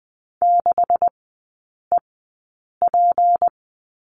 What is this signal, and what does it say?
Morse code, heard as a keyed tone, about 20 wpm, spaced as for 8 wpm, 700 Hz. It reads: 6EP
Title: 218 Lesson 35e - Number 0 - Three Characters 20wpm